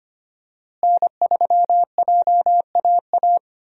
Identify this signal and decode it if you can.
Morse code, heard as a keyed tone, about 25 wpm, 700 Hz. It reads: N3JAA
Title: USA Call Signs - 5 Characters - Repeated 2x - 25wpm